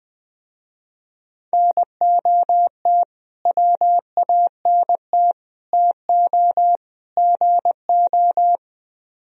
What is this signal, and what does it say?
Morse code, heard as a keyed tone, about 22 wpm, 700 Hz. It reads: NOT WANT TO GO
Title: Speed-Racing - Sets of 4 Words 20wpm